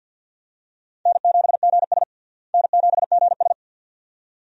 Morse code, heard as a keyed tone, about 46 wpm, 700 Hz. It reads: N7GS N7GS
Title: USA Call Signs - 4 Characters - Repeated 2x - 50wpm